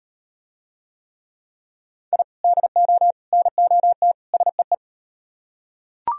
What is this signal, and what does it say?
Morse code, heard as a keyed tone, about 38 wpm, 700 Hz. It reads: I DO NOT SEE